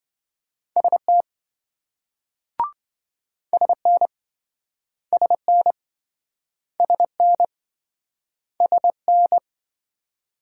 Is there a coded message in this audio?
Yes: Morse code from 0.8 to 9.4 s, heard as a keyed tone, about 26 wpm, 700 Hz. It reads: ST SN SN SN SN